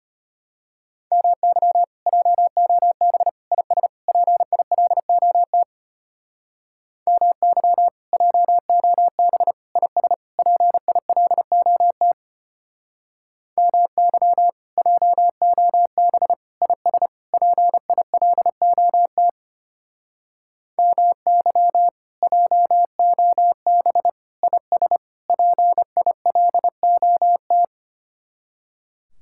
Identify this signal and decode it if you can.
Morse code, heard as a keyed tone, about 31 wpm, 700 Hz. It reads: MY JOB IS PILOT MY JOB IS PILOT MY JOB IS PILOT MY JOB IS PILOT
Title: Speed-Racing Common QSO Phrases Encore - 25wpm